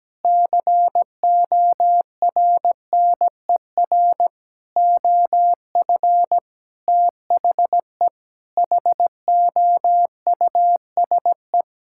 Morse code, heard as a keyed tone, about 17 wpm, 700 Hz. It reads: CORNER OF THE HOUSE
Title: Sentences from Top 1600 Words 17wpm